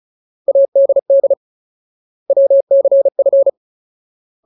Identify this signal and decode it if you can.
Morse code, heard as a keyed tone, about 34 wpm, 550 Hz. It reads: ADD WCF
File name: Ultimate Word List / 35WPM Ultimate